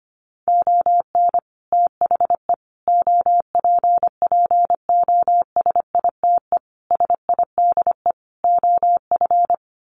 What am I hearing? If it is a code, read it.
Morse code, heard as a keyed tone, about 25 wpm, 700 Hz. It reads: ON THE OPPOSITE SIDE OF
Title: Sets of 5 Words Encore2 25wpm